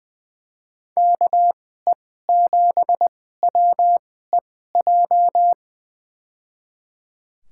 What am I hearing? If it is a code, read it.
Morse code, heard as a keyed tone, about 21 wpm, spaced as for 15 wpm, 700 Hz. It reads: KE7WEJ
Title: USA Call Signs - 6 Characters - 2x Character Spacing - 20wpm